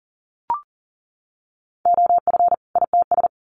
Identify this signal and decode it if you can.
Morse code, heard as a keyed tone, about 40 wpm, 700 Hz. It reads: OF ITS
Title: Sets of 2 Words 40wpm